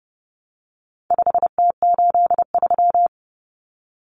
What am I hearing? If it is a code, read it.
Morse code, heard as a keyed tone, about 30 wpm, 700 Hz. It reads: 5T83
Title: Common Cut Numbers - 4-Digit 30wpm